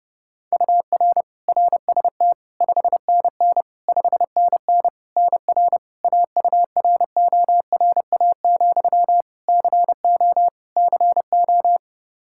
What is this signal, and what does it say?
Morse code, heard as a keyed tone, about 30 wpm, 700 Hz. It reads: UR RST 5NN 5NN NR AURORA, CO CO
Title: Practice QSO - Exchange 1 - Beginning - 30wpm